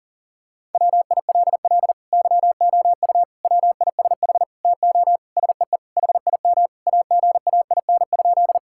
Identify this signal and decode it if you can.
Morse code, heard as a keyed tone, about 40 wpm, 700 Hz. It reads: WILL YOU WISH TO SEE HIM AGAIN?
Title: Sentences from Top 1600 Words 40wpm